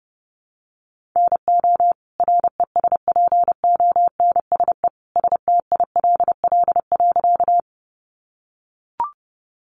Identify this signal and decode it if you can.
Morse code, heard as a keyed tone, about 30 wpm, 700 Hz. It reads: NO RESPONSE STILL.
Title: Sentences from Top 1400 Words 30wpm